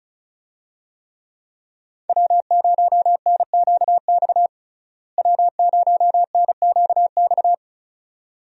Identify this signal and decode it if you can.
Morse code, heard as a keyed tone, about 34 wpm, 700 Hz. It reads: W0NQX W0NQX
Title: USA Call Signs - 5 Characters - Repeated 2x - 35wpm